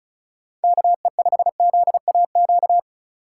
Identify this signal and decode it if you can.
Morse code, heard as a keyed tone, about 36 wpm, 700 Hz. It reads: KE5ZAQ
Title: USA Call Signs - 3x Character Spacing - 35wpm